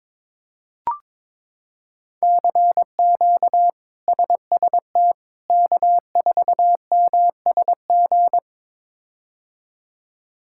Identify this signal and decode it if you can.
Morse code, heard as a keyed tone, about 22 wpm, 700 Hz. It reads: CQ SST K4MSG
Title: K1USN SST QSO Practice - 22wpm